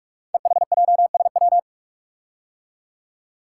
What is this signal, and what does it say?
Morse code, heard as a keyed tone, about 44 wpm, 700 Hz. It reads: EHJSW
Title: Code Groups - 5 Letters - 45wpm